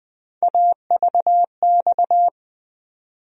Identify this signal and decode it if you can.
Morse code, heard as a keyed tone, about 20 wpm, 700 Hz. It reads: AVX
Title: USA Call Sign Postfixes - 3 Letters 20wpm